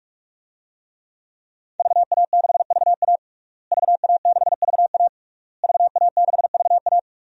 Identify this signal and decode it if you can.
Morse code, heard as a keyed tone, about 42 wpm, 700 Hz. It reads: VA6VA VA6VA VA6VA
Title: DXCC-100 Call Signs - 5 Characters - Repeated 3x - 45wpm